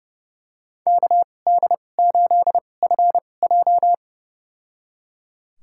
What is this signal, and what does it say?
Morse code, heard as a keyed tone, about 31 wpm, spaced as for 22 wpm, 700 Hz. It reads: KD8FJ
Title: USA Call Signs - 5 Characters - 2x Character Spacing - 30wpm